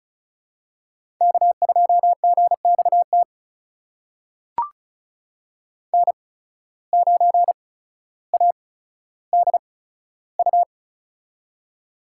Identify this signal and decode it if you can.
Morse code, heard as a keyed tone, about 36 wpm, spaced as for 15 wpm, 700 Hz. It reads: K2GXT N9ADU
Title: USA Call Signs - 5 Characters - 8x Character Spacing - 35wpm